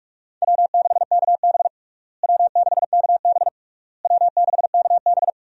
Morse code, heard as a keyed tone, about 46 wpm, 700 Hz. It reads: W6KB W6KB W6KB
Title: USA Call Signs - 4 Characters - Repeated 3x - 45wpm